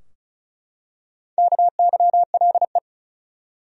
Morse code, heard as a keyed tone, about 36 wpm, 700 Hz. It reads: KYLE
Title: USA Common Names - 35wpm